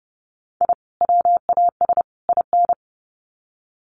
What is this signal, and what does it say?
Morse code, heard as a keyed tone, about 30 wpm, 700 Hz. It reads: I WAS IN
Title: Sets of 3 Words 30wpm